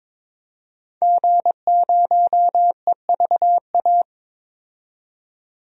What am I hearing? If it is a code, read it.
Morse code, heard as a keyed tone, about 22 wpm, 700 Hz. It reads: G0EVA
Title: DXCC-100 Call Signs - 5 Characters - 22wpm